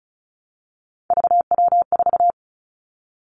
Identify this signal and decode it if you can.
Morse code, heard as a keyed tone, about 36 wpm, 700 Hz. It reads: VW4
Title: Three Challenging Letter-Number Combinations 35wpm